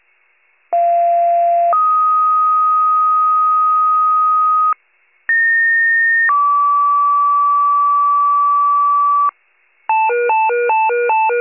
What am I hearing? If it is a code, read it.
Two-tone page, heard as a one-second tone, then a three-second tone, about 690 then 1250 Hz, about 1850 then 1150 Hz. It is heard from 0.7 to 9.3 s.
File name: pager / 94 RESCUE DUTY / wcalert